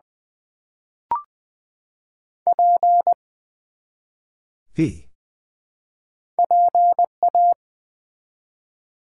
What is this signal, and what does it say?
Morse code, heard as a keyed tone, about 20 wpm, 700 Hz. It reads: P PA
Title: Build A Word - 1 Letter At A Time - 3 to 5-Letter Words - 20wpm